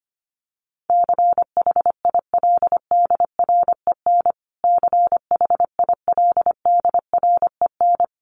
Morse code, heard as a keyed tone, about 25 wpm, 700 Hz. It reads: CHILDREN CHILDREN
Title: Top 500 Words Repeated 25wpm